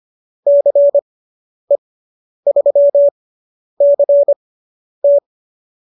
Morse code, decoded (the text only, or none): CE3CT